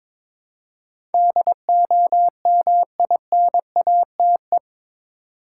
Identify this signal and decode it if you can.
Morse code, heard as a keyed tone, about 21 wpm, 700 Hz. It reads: DOMINATE
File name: practice / 22wpm / Eight Letter Words - 22wpm